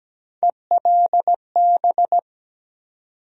Morse code, heard as a keyed tone, about 17 wpm, 700 Hz. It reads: ELB
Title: USA Call Sign Postfixes - 3 Letters 17wpm